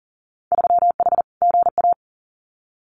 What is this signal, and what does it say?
Morse code, heard as a keyed tone, about 40 wpm, 700 Hz. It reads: 3H GA
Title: Winter Field Day Exchanges Encore 40wpm